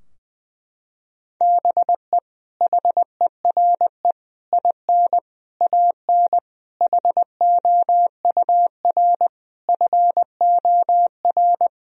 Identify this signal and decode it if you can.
Morse code, heard as a keyed tone, about 21 wpm, 700 Hz. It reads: BE HERE IN AN HOUR FOR
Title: Sentences from Top 1000 Words - Review Part 4 - 20wpm